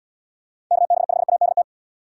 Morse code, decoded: KB5ADT

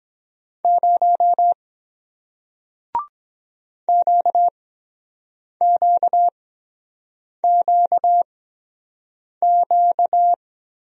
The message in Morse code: 0 QQQQ

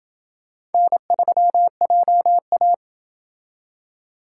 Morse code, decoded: N3JA

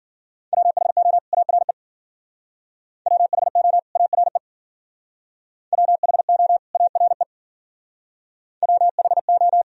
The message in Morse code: WHO ARE WHO ARE WHO ARE WHO

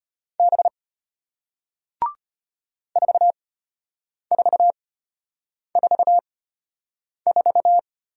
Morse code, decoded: B 4444

